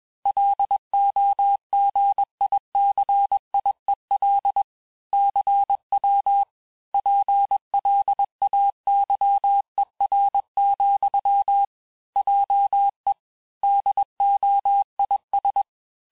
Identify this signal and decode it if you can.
Morse code, heard as a keyed tone, about 21 wpm, 800 Hz. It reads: LOGICIEL CW PLAYER, JE DOIS
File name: teste-FIN-de-QSO-vitesse-20mpm